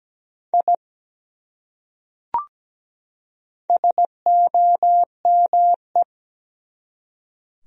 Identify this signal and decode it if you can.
Morse code, heard as a keyed tone, about 17 wpm, 700 Hz. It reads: I SOME
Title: Top 100 Words 17wpm